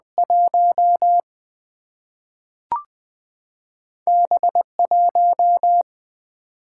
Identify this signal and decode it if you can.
Morse code, heard as a keyed tone, about 20 wpm, 700 Hz. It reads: 1 B 1
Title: Single-Letter Single-Number 20wpm